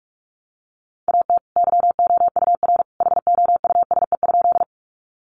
Morse code, heard as a keyed tone, about 44 wpm, 700 Hz. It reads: AT YOUR HOUSE?
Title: Sentences from Top 200 Words 45wpm